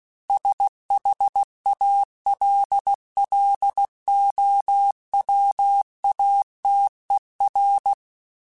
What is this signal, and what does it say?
Morse code, heard as a keyed tone, about 16 wpm, 800 Hz. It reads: SHALLOWATER
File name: fromSMs / FromSMs mp3s / Cities@15